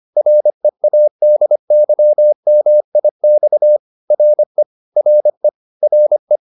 Morse code, heard as a keyed tone, about 25 wpm, 600 Hz. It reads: READYMIX RE RE RE